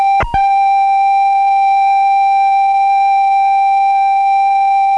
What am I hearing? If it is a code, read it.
Computer data on cassette.